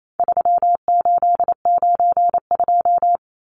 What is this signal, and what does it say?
Morse code, heard as a keyed tone, about 28 wpm, 700 Hz. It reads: 3892
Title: Numbers - 4-Digit 28wpm